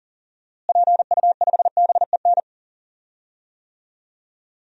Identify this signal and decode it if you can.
Morse code, heard as a keyed tone, about 40 wpm, 700 Hz. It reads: PU5BEN
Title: DXCC-100 Call Signs - 6 Characters - Repeated 3x - 40wpm